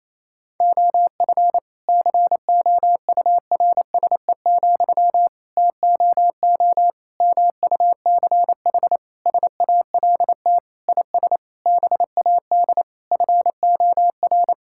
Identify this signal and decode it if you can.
Morse code, heard as a keyed tone, about 27 wpm, 700 Hz. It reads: OF COURSE, TOO MUCH SALT IS BAD FOR
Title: Sentences from Top 1000 Words - Review Part 5 - 28wpm